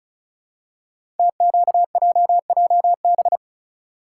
Morse code, heard as a keyed tone, about 36 wpm, 700 Hz. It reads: TQJJB
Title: Code Groups - 5 Letter:Numbers - 35wpm